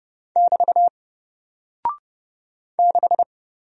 Morse code, heard as a keyed tone, about 30 wpm, 700 Hz. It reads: <BT> 6
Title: Single Character (Letter:Number:Punctuation:Prosign) Alt 30wpm